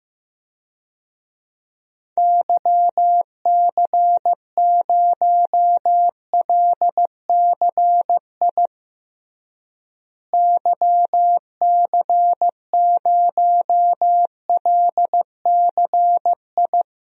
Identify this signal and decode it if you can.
Morse code, heard as a keyed tone, about 16 wpm, 700 Hz. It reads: YC0LCI YC0LCI
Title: DXCC-100 Call Signs - 6 Characters - Repeated 3x - 15wpm